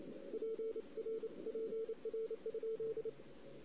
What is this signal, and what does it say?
Morse code, heard as a keyed tone, about 29 wpm, 440 Hz. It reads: PR PR?
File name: CW / 2008-ss-cw-k5tr-op-n5rz -16:00